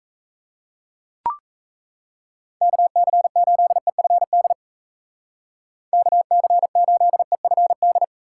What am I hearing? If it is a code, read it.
Morse code, heard as a keyed tone, about 35 wpm, 700 Hz. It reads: KC8EFD KC8EFD
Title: Speed Racing - USA Call Signs 28wpm